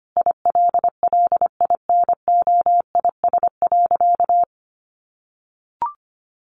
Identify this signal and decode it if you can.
Morse code, heard as a keyed tone, about 25 wpm, 700 Hz. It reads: ILLINOIS.
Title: A Sound Of Thunder 25wpm